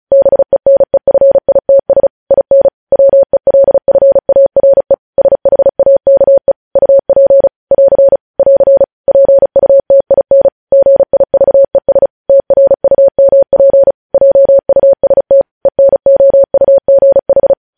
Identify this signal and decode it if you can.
Morse code, heard as a keyed tone, about 36 wpm, 550 Hz. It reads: BENEFITS IN WELFARE SHAKE UP <AR> <AR> PUTIN GIVES TRUMP JUST ENOUGH
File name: podcast / News headlines-1x-35x35